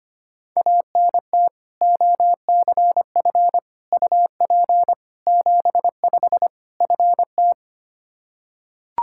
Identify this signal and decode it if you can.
Morse code, heard as a keyed tone, about 25 wpm, 700 Hz. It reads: ANT OCF UP 75 FT
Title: Practice QSO - Exchange 2 - Info - 25wpm